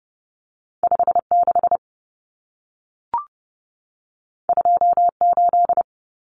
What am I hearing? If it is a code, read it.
Morse code, heard as a keyed tone, about 30 wpm, 700 Hz. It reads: 56 28
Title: Numbers - 2-Digit 30wpm